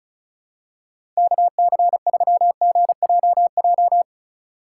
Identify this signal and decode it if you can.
Morse code, heard as a keyed tone, about 34 wpm, 700 Hz. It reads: KC3GJJ
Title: USA Call Signs - 6 Characters - 4x Character Spacing - 35wpm